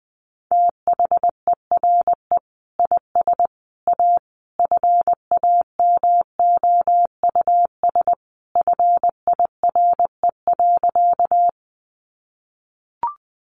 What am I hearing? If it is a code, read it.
Morse code, heard as a keyed tone, about 20 wpm, 700 Hz. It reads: THERE IS A FAMOUS FIRE.
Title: Sentences from Top 1500 Words 20wpm